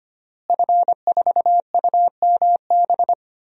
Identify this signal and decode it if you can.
Morse code, heard as a keyed tone, about 25 wpm, 700 Hz. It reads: F4UMB
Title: DXCC-100 Call Signs - 5 Characters - Repeated 3x - 25wpm